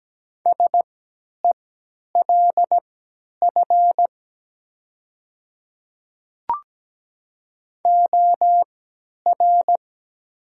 Morse code, heard as a keyed tone, about 17 wpm, spaced as for 10 wpm, 700 Hz. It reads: SELF OR
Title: 3x Character Spacing - Top 300 Words - 17wpm